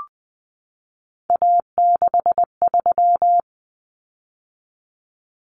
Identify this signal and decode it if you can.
Morse code, heard as a keyed tone, about 20 wpm, 700 Hz. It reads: A63
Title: Common Cut Numbers - 3-Digit 20wpm